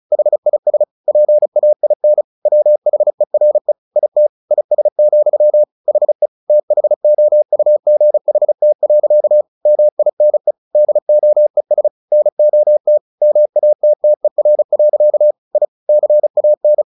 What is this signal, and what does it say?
Morse code, heard as a keyed tone, about 34 wpm, 600 Hz. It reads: HIS PAIN WHERE IT IS, HE THOUGHT. MINE DOES NOT MATTER. I CAN
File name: the old man and the sea 2 35wpm